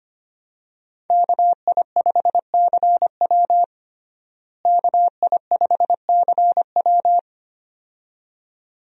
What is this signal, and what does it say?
Morse code, heard as a keyed tone, about 25 wpm, 700 Hz. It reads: KI5CW KI5CW
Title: USA Call Signs - 5 Characters - Repeated 2x - 25wpm